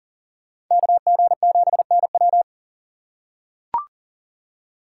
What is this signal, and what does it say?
Morse code, heard as a keyed tone, about 40 wpm, 700 Hz. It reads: KG7NW
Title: USA Call Signs - 2x Character Spacing - 40wpm